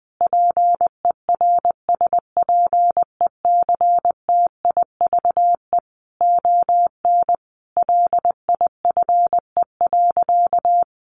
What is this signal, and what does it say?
Morse code, heard as a keyed tone, about 20 wpm, 700 Hz. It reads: PERSPECTIVE ON LIFE.
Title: Sentences from Top 1200 Words 20wpm